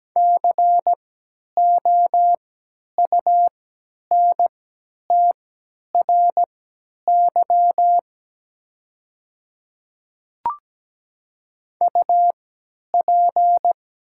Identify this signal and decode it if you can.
Morse code, heard as a keyed tone, about 17 wpm, spaced as for 10 wpm, 700 Hz. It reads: COUNTRY UP
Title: 3x Character Spacing - Top 500 Words - 17wpm